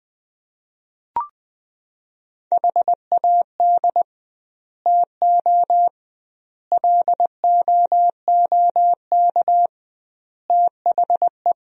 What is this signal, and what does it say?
Morse code, heard as a keyed tone, about 20 wpm, 700 Hz. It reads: HAD TO LOOK THE
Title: 2x Word-Spacing - Sentences from Top 100 Words - 20wpm